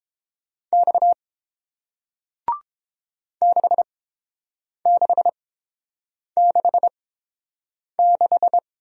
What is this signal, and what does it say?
Morse code, heard as a keyed tone, about 30 wpm, spaced as for 7 wpm, 700 Hz. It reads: X 6666